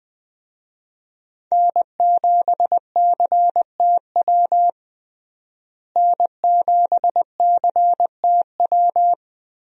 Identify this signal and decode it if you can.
Morse code, heard as a keyed tone, about 21 wpm, 700 Hz. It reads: N7CTW N7CTW
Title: USA Call Signs - 5 Characters - Repeated 3x - 20wpm